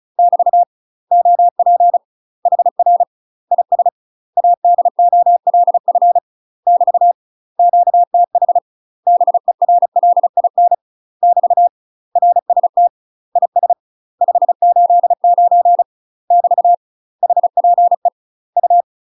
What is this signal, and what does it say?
Morse code, heard as a keyed tone, about 34 wpm, 700 Hz. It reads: <BT> OP HR IS ADOLF <BT> QTH BERLIN <BT> RST IS 589 <BT> HPE U
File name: UK MASTER QSOs TRAINING / 35WPM / UK MASTER QSOs 0034